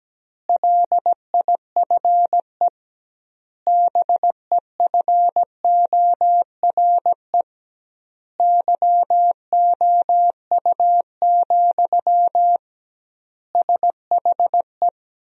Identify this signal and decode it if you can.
Morse code, heard as a keyed tone, about 17 wpm, 700 Hz. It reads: LIFE BEFORE YOU, SHE
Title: 2x Word-Spacing - Sentences from Top 400 Words - 17wpm